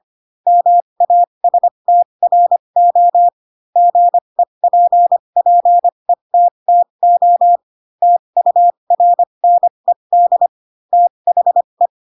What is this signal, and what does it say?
Morse code, heard as a keyed tone, about 25 wpm, spaced as for 22 wpm, 700 Hz. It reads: MASTRO GEPPETTO TURNED THE
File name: Pinocchio / Pinocchio ch2 18 25 ik1hge